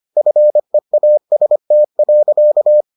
Morse code, heard as a keyed tone, about 25 wpm, 600 Hz. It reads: FEAST.